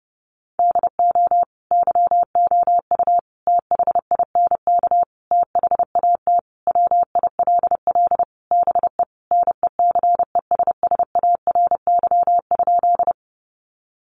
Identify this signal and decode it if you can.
Morse code, heard as a keyed tone, about 30 wpm, 700 Hz. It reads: DO YOU THINK THAT WILL BE NECESSARY?
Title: Sentences from Top 800 Words 30wpm